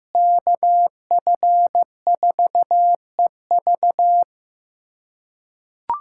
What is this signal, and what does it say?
Morse code, heard as a keyed tone, about 15 wpm, 700 Hz. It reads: KF4EV
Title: USA Callsign DE Callsign - 5 Characters - 15wpm